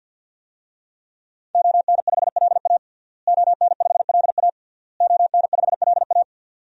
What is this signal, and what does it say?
Morse code, heard as a keyed tone, about 52 wpm, 700 Hz. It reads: ON5LA ON5LA ON5LA
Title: DXCC-100 Call Signs - 5 Characters - Repeated 3x - 50wpm